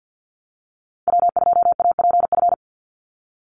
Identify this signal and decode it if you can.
Morse code, heard as a keyed tone, about 50 wpm, 700 Hz. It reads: W2APF
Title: Call Signs Repeated 3x - 50wpm